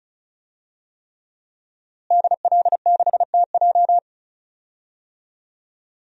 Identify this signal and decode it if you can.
Morse code, heard as a keyed tone, about 34 wpm, 690 Hz. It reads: DL6TJ
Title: DXCC-100 Call Signs - 5 Characters - Repeated 3x - 35wpm